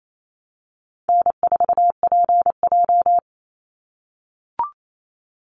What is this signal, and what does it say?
Morse code, heard as a keyed tone, about 28 wpm, 700 Hz. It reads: N4PJ
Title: Call Signs - 28wpm